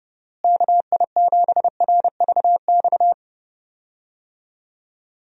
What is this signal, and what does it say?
Morse code, heard as a keyed tone, about 32 wpm, 700 Hz. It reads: KI7RVX
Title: USA Call Signs - 6 Characters - Repeated 2x - 30wpm